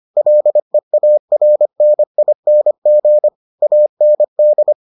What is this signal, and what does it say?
Morse code, heard as a keyed tone, about 25 wpm, 600 Hz. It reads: LEARNING AND